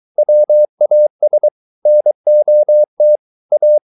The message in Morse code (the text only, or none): WAS NOT A